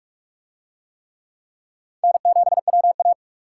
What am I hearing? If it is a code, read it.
Morse code, heard as a keyed tone, about 40 wpm, 700 Hz. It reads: N7WA